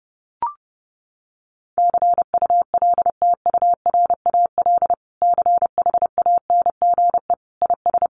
Morse code, heard as a keyed tone, about 30 wpm, 700 Hz. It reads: CULTURAL CHANGE IS
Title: Sentences from Top 1500 Words 30wpm